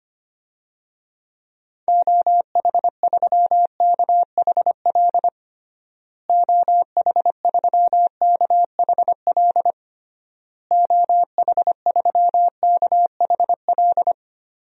Morse code, heard as a keyed tone, about 25 wpm, 700 Hz. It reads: OH3KHL OH3KHL OH3KHL
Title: DXCC-100 Call Signs - 6 Characters - Repeated 3x - 25wpm